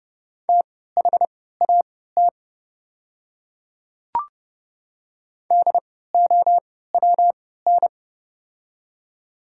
Morse code, heard as a keyed tone, about 30 wpm, spaced as for 17 wpm, 700 Hz. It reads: THAT DOWN
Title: 3x Character Spacing - Top 100 Words - 30wpm